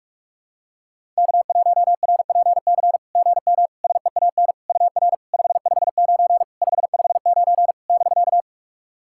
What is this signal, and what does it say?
Morse code, heard as a keyed tone, about 44 wpm, 690 Hz. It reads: K1RPC GM SEAN UR 559 559 <BK>
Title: SOTA QSO Practice 45wpm